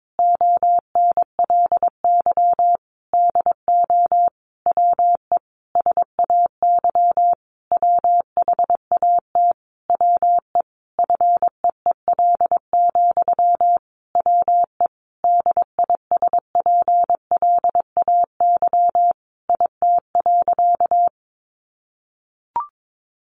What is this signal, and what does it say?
Morse code, heard as a keyed tone, about 22 wpm, 700 Hz. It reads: ONLY DO WE SAY WHAT WE FEEL, WE DISPLAY IT.
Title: Sentences from Top 1200 Words 22wpm